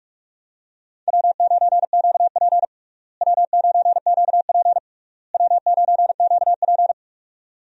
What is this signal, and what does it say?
Morse code, heard as a keyed tone, about 44 wpm, 700 Hz. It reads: W9QP W9QP W9QP
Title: USA Call Signs - 4 Characters - Repeated 3x - 45wpm